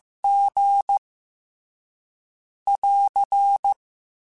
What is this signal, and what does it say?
Morse code, heard as a keyed tone, about 15 wpm, 800 Hz. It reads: G <AR>
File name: Special INGs / Special INGs-mp3s / Slow ing @13